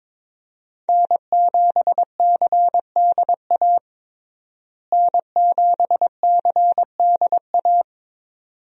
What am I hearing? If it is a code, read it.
Morse code, heard as a keyed tone, about 22 wpm, 700 Hz. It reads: N7CDA N7CDA
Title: USA Call Signs - 5 Characters - Repeated 2x - 22wpm